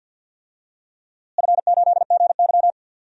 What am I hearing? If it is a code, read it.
Morse code, heard as a keyed tone, about 52 wpm, 700 Hz. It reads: F8GY
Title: DXCC-100 Call Signs - 4 Characters - Repeated 2x - 50wpm